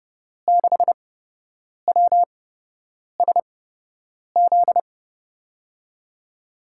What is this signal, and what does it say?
Morse code, heard as a keyed tone, about 30 wpm, spaced as for 8 wpm, 700 Hz. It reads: 6WSZ
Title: USA Call Signs - 8x Character Spacing - 30wpm